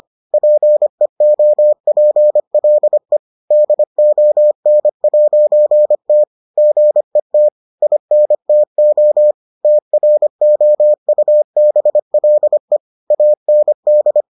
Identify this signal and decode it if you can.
Morse code, heard as a keyed tone, about 25 wpm, 600 Hz. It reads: PEOPLE DON'T GET INTO TROUBLE AND